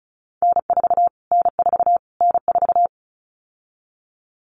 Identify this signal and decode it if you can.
Morse code, heard as a keyed tone, about 34 wpm, 700 Hz. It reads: N4 N4 N4
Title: Callsign Prefixes for Top 25 DXCC - Repeated 3x - 35wpm